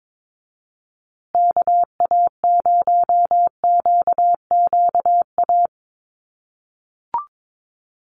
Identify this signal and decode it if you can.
Morse code, heard as a keyed tone, about 22 wpm, 700 Hz. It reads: KA0QQA
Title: USA Call Signs Repeated 3x 22wpm